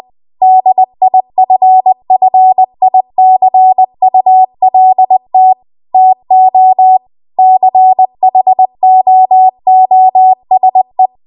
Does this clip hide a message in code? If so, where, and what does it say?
Yes: Morse code from 0.4 to 11.1 s, heard as a keyed tone, about 20 wpm, 750 Hz. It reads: DIFFICULT TO CHOOSE